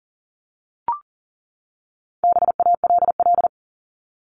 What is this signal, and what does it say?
Morse code, heard as a keyed tone, about 40 wpm, 700 Hz. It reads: BALL
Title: Top 751-1000 Words 40wpm